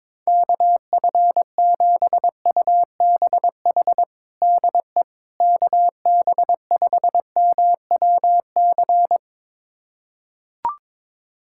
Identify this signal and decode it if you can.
Morse code, heard as a keyed tone, about 22 wpm, 700 Hz. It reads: KF7UBH DE KB5MWC
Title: USA Callsign DE Callsign - 6 Characters - 22wpm